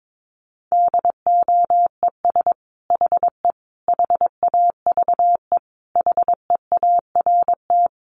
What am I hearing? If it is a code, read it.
Morse code, heard as a keyed tone, about 22 wpm, 700 Hz. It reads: DOES HE HAVE HEART
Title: Sentences from Top 1200 Words 22wpm